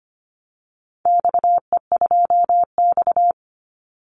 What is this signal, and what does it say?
Morse code, heard as a keyed tone, about 25 wpm, 700 Hz. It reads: XE2X
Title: Call Signs Repeated 3x - 25wpm